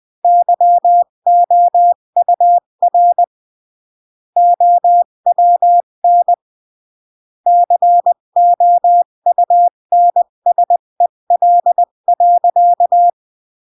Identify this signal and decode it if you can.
Morse code, heard as a keyed tone, about 20 wpm, 700 Hz. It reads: YOUR OWN COUNSEL.
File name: CW / jodorowsky / 82 commandments 20-15